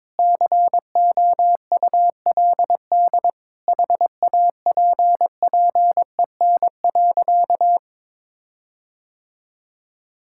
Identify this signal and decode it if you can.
Morse code, heard as a keyed tone, about 22 wpm, 700 Hz. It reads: COULD HAPPEN.
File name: Sentences with 9-Word Phrases - 22wpm